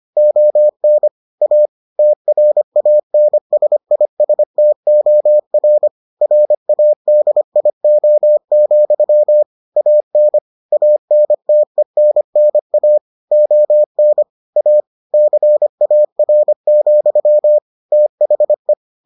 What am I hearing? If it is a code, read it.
Morse code, heard as a keyed tone, about 25 wpm, 600 Hz. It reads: ON A TRANSISTOR RADIO, AN ANTENNA ON A CAR, THE